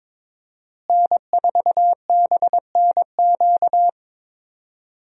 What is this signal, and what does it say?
Morse code, heard as a keyed tone, about 22 wpm, 700 Hz. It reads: N4BNQ
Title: USA Call Signs - 5 Characters - Repeated 2x - 22wpm